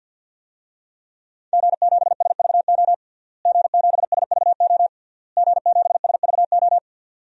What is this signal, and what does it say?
Morse code, heard as a keyed tone, about 50 wpm, 700 Hz. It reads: G7SVO G7SVO G7SVO
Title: DXCC-100 Call Signs - 5 Characters - Repeated 3x - 50wpm